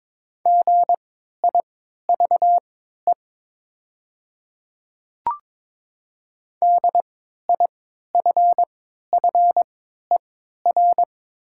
Morse code, heard as a keyed tone, about 22 wpm, spaced as for 12 wpm, 700 Hz. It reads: GIVE DIFFER
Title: 3x Character Spacing - Top 500 Words - 22wpm